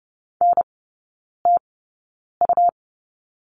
Morse code, heard as a keyed tone, about 30 wpm, spaced as for 9 wpm, 700 Hz. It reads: NTU
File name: Lesson 18e - Letter m - Three Characters 30wpm